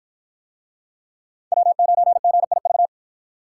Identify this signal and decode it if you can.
Morse code, heard as a keyed tone, about 52 wpm, 700 Hz. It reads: W9GIV